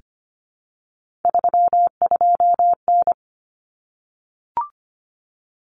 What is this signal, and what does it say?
Morse code, heard as a keyed tone, about 25 wpm, 700 Hz. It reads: 32N
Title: Common Cut Numbers - 3-Digit 25wpm